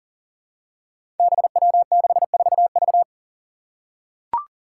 Morse code, heard as a keyed tone, about 40 wpm, 700 Hz. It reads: BW64V
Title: Five Challenging Letter-Number Combinations 40wpm